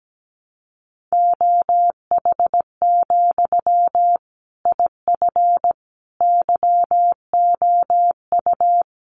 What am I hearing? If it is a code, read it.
Morse code, heard as a keyed tone, about 17 wpm, 700 Hz. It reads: OH, IF YOU
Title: Sentences from Top 1100 Words 17wpm